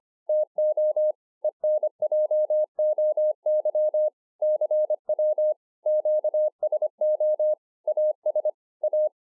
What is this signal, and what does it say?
Morse code, heard as a keyed tone, about 26 wpm, 610 Hz. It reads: TO ENJOY CW QSO AS A
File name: 2011-04-mp3 / BEGINNERS DIARY 25WPM